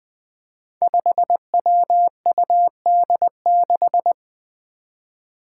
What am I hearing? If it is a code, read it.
Morse code, heard as a keyed tone, about 20 wpm, 700 Hz. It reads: 5WUD6
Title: Five Challenging Letter-Number Combinations 20wpm